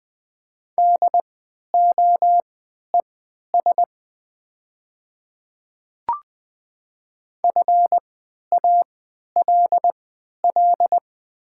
Morse code, heard as a keyed tone, about 20 wpm, spaced as for 11 wpm, 700 Hz. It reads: DOES FALL
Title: 3x Character Spacing - Top 500 Words - 20wpm